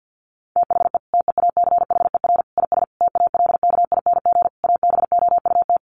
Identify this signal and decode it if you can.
Morse code, heard as a keyed tone, about 50 wpm, 700 Hz. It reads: THE TEACHER IS TALKING ABOUT
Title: Easy Phrases 50wpm